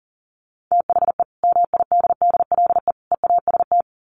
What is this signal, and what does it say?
Morse code, heard as a keyed tone, about 37 wpm, 700 Hz. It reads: THE MIDDLE EAST
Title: Sets of 3 Words 40wpm